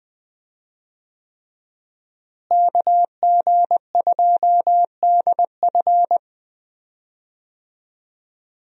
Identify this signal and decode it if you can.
Morse code, heard as a keyed tone, about 20 wpm, 700 Hz. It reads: KG2DF
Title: USA Call Signs - 5 Characters - Repeated 3x - 20wpm